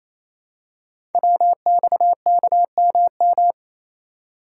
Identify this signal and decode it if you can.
Morse code, heard as a keyed tone, about 27 wpm, 700 Hz. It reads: WXKMM
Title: Code Groups - 5 Letters - 28wpm